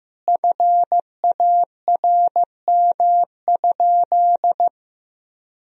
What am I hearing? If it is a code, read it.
Morse code, heard as a keyed tone, about 15 wpm, 700 Hz. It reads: FARM?
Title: Sentences from Top 1000 Words - Review Part 3 - 15wpm